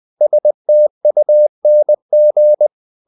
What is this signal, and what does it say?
Morse code, heard as a keyed tone, about 20 wpm, 600 Hz. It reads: STUNG
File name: the old man and the sea part 2 20wpm